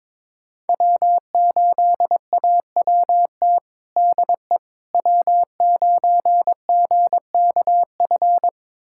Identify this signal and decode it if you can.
Morse code, heard as a keyed tone, about 22 wpm, 700 Hz. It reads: W8AWT DE W9GKF
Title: USA Callsign DE Callsign - 5 Characters - 22wpm